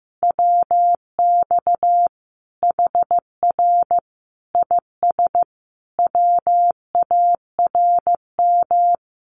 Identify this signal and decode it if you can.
Morse code, heard as a keyed tone, about 15 wpm, 700 Hz. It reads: WX HR IS WARM
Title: Common QSO Phrases 15wpm